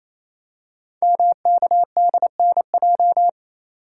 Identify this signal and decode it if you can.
Morse code, heard as a keyed tone, about 29 wpm, 700 Hz. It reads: MKDNJ
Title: Code Groups - 5 Letters - 28wpm